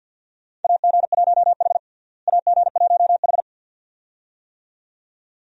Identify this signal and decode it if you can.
Morse code, heard as a keyed tone, about 52 wpm, 700 Hz. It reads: AG1H AG1H
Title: USA Call Signs - 4 Characters - Repeated 2x - 50wpm